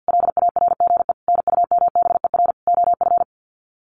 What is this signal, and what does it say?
Morse code, heard as a keyed tone, about 50 wpm, 700 Hz. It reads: LARGE NUMBER OF
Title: Sets of 4 Words 50wpm